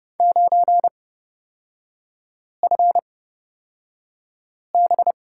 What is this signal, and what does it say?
Morse code, heard as a keyed tone, about 29 wpm, spaced as for 5 wpm, 690 Hz. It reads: 9FB
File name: Lesson 37e - Letter x - Three Characters 30wpm